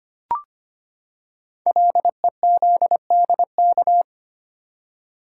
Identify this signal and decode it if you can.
Morse code, heard as a keyed tone, about 25 wpm, 700 Hz. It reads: LEZDK
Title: Code Groups - 5 Letters - 25wpm